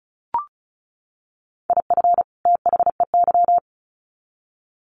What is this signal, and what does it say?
Morse code, heard as a keyed tone, about 34 wpm, 700 Hz. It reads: IF THEY
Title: Sets of 2 Words 35wpm